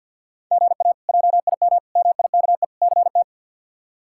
Morse code, heard as a keyed tone, about 47 wpm, 700 Hz. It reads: GA JIM MIKE CT
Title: K1USN SST QSO Practice - 50wpm